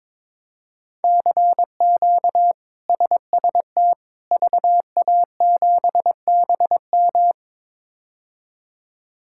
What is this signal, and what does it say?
Morse code, heard as a keyed tone, about 22 wpm, 700 Hz. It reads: CQ SST VA7BM
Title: K1USN SST QSO Practice - 22wpm